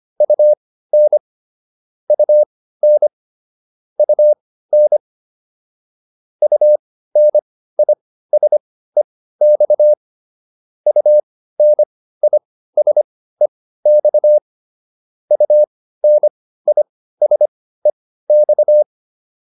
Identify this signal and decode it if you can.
Morse code, heard as a keyed tone, about 25 wpm, spaced as for 15 wpm, 600 Hz. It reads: UN UN UN UNISEX UNISEX UNISEX